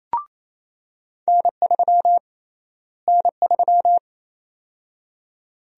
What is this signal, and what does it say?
Morse code, heard as a keyed tone, about 28 wpm, 700 Hz. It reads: N3 N3
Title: USA Call Sign Prefixes - Repeated 2x - 28wpm